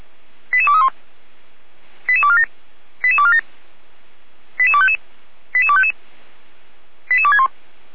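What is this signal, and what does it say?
Five-tone selective call (ZVEI), heard as quick runs of five tones, tones of about 70 ms. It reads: 88231 88247 88247 88255 88255 88261
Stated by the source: ZVEI five-tone selective call received with an SDR.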